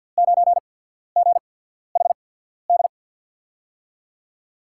Morse code, heard as a keyed tone, about 53 wpm, spaced as for 14 wpm, 700 Hz. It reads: 9GHD
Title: USA Call Signs - 5 Characters - 8x Character Spacing - 50wpm